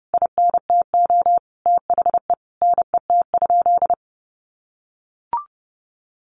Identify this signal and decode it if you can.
Morse code, heard as a keyed tone, about 30 wpm, 700 Hz. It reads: INTO THE NET?
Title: Sentences from Top 1400 Words 30wpm